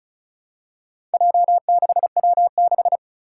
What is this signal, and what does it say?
Morse code, heard as a keyed tone, about 36 wpm, 700 Hz. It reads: J6W6
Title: Four Challenging Letter-Number Combinations 35wpm